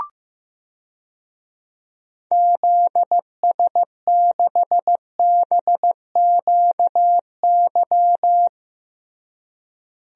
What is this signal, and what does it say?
Morse code, heard as a keyed tone, about 15 wpm, 700 Hz. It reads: ZS6BQY
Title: DXCC-100 Call Signs - 6 Characters - Repeated 3x - 15wpm